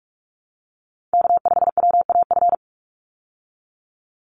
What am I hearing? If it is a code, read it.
Morse code, heard as a keyed tone, about 44 wpm, 700 Hz. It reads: K5WAF
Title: Call Signs Repeated 2x - 45wpm